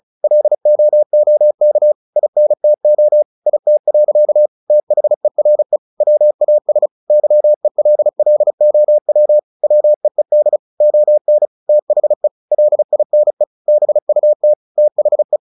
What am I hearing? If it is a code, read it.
Morse code, heard as a keyed tone, about 35 wpm, 600 Hz. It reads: LOOK INTO IT. THERE WAS YELLOW WEED ON THE LINE BUT THE
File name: the old man and the sea / the old man and the sea 1 35wpm